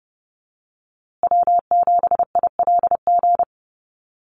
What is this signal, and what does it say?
Morse code, heard as a keyed tone, about 30 wpm, 700 Hz. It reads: W7ILG
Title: USA Call Signs Repeated 3x 30wpm